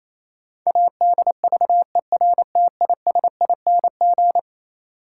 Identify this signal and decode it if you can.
Morse code, heard as a keyed tone, about 28 wpm, 700 Hz. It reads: ADVERTISING
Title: Sentences from Top 1000 Words - Review Part 6 - New Word Drill - 28wpm